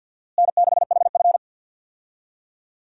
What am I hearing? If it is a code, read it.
Morse code, heard as a keyed tone, about 52 wpm, 700 Hz. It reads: N6HF
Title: USA Call Signs - 4 Characters - Repeated 2x - 50wpm